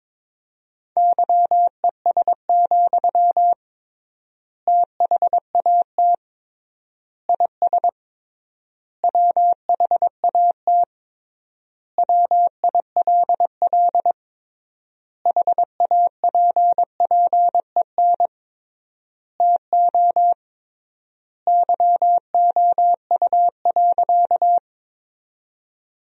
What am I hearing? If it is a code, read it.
Morse code, heard as a keyed tone, about 22 wpm, 700 Hz. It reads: YES, THAT IS WHAT WILL HAPPEN TO YOU.
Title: 3x Word-Spacing - Sentences from Top 500 Words - 22wpm